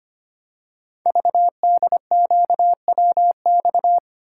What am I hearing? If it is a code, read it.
Morse code, heard as a keyed tone, about 25 wpm, 700 Hz. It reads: VDQWX
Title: Code Groups - 5 Letter:Numbers - 25wpm